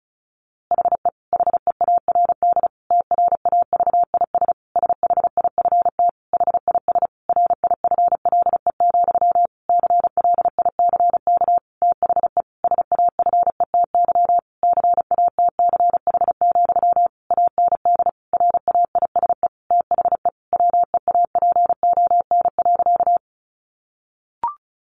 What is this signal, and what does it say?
Morse code, heard as a keyed tone, about 34 wpm, 700 Hz. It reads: HE HEARD TRAVIS SHIFT HIS RIFLE, CLICK THE SAFETY CATCH, AND RAISE THE WEAPON.
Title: A Sound Of Thunder 35wpm